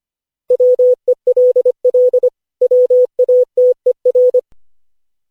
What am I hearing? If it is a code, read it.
Morse code, heard as a keyed tone, about 25 wpm, 500 Hz. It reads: WELL WATER
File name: cw / Session5 cw training